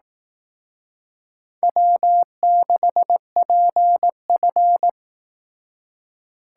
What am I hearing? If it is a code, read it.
Morse code, heard as a keyed tone, about 18 wpm, 700 Hz. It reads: W6PF
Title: Speed Racing - USA 1x2 Call Signs 15wpm